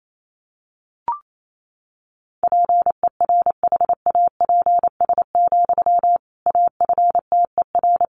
Morse code, heard as a keyed tone, about 28 wpm, 700 Hz. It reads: PERHAPS, AFTER
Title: Sentences from Top 500 Words 28wpm